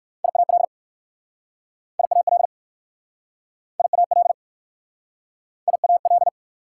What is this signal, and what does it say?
Morse code, heard as a keyed tone, about 60 wpm, 690 Hz. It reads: IAL IAL IAL IAL